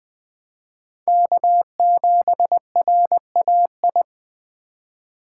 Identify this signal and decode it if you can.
Morse code, heard as a keyed tone, about 20 wpm, 700 Hz. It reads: K7RAI
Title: Speed Racing - USA 1x3 Call Signs 17wpm